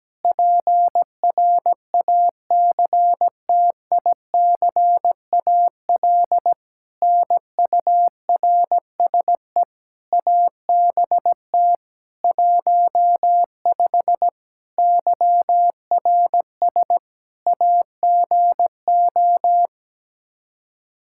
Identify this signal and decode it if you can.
Morse code, heard as a keyed tone, about 17 wpm, 700 Hz. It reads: PRACTICAL NURSE ABT 15 YRS AGO
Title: K1USN SST QSO Practice 17wpm